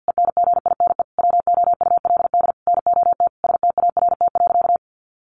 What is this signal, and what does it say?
Morse code, heard as a keyed tone, about 50 wpm, 700 Hz. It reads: ENGINE WOULD NOT START.
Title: Sentences from Top 1500 Words 50wpm